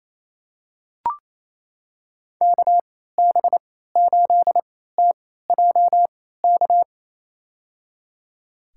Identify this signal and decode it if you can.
Morse code, heard as a keyed tone, about 28 wpm, spaced as for 16 wpm, 700 Hz. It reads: KB8TJK
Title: USA Call Signs - 6 Characters - 3x Character Spacing - 28wpm